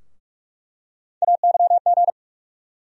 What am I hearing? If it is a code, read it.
Morse code, heard as a keyed tone, about 46 wpm, 700 Hz. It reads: AYG
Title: USA Call Sign Postfixes - 3 Letters 45wpm